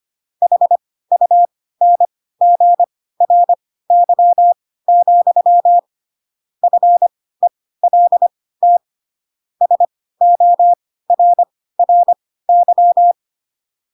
Morse code, heard as a keyed tone, about 25 wpm, spaced as for 16 wpm, 700 Hz. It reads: HUNGRY, FELT SORRY
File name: Pinocchio / Pinocchio ch7 10 25 ik1hge